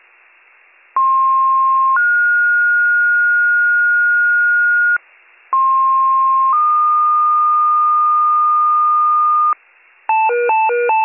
Two-tone page, heard as a one-second tone, then a three-second tone, about 1060 then 1430 Hz, about 1060 then 1220 Hz, with noise. It is heard from 1.0 to 9.5 s.